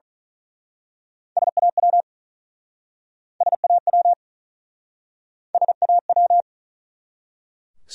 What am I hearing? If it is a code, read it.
Morse code, heard as a keyed tone, about 46 wpm, 700 Hz. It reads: SAW SAW SAW